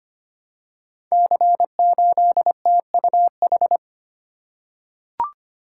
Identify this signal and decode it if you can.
Morse code, heard as a keyed tone, about 25 wpm, 700 Hz. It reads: C8TUH
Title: Code Groups - 5 Letter:Numbers - 25wpm